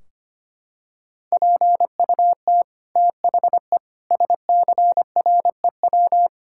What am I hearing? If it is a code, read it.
Morse code, heard as a keyed tone, about 25 wpm, 700 Hz. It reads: PUT THE SCREW
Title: Sentences from Top 1600 Words 25wpm